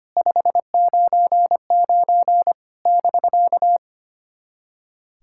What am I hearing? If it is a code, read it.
Morse code, heard as a keyed tone, about 25 wpm, 700 Hz. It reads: 599 <BK>
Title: SOTA QSO Practice 25wpm